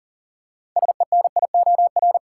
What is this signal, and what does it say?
Morse code, heard as a keyed tone, about 40 wpm, 700 Hz. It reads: SENIOR
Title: Six Letter Words - 40wpm